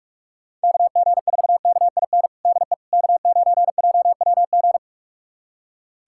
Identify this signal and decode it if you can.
Morse code, heard as a keyed tone, about 46 wpm, 700 Hz. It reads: KG4KIN DE K9JWG
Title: USA Callsign DE Callsign 45wpm